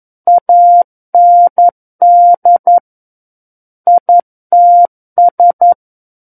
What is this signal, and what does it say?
Morse code, heard as a keyed tone, about 11 wpm, 700 Hz. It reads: AND ITS
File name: 11 WPM